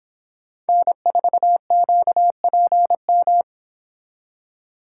Morse code, heard as a keyed tone, about 29 wpm, 700 Hz. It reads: N4QPM